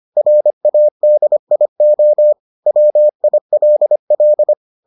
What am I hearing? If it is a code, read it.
Morse code, heard as a keyed tone, about 25 wpm, 600 Hz. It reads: RADIO WILL